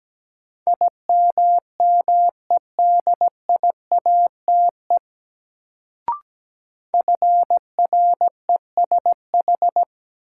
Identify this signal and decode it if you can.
Morse code, heard as a keyed tone, about 17 wpm, 700 Hz. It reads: IMMEDIATE FRESH
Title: Sentences from Top 1200 Words - New Word Drill Alt - 17wpm